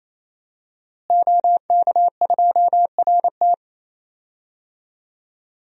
Morse code, heard as a keyed tone, about 27 wpm, 700 Hz. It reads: OK2RT
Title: DXCC-100 Call Signs - 5 Characters - Repeated 3x - 28wpm